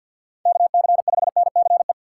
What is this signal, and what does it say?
Morse code, heard as a keyed tone, about 47 wpm, 700 Hz. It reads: KC5NCE
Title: USA Call Signs - 6 Characters - 4x Character Spacing - 50wpm